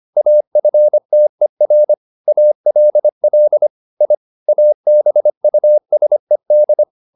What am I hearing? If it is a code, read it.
Morse code, heard as a keyed tone, about 25 wpm, 600 Hz. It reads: AFTER ALL I ABUSED
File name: the old man and the sea part 1 25wpm